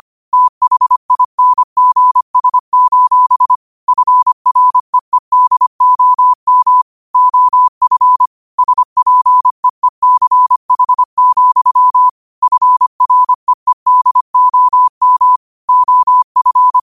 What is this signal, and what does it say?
Morse code, heard as a keyed tone, about 25 wpm, 1000 Hz. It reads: THINGS: FREEDOM OF SPEECH, FREEDOM OF
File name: qotdfiles / 25wpm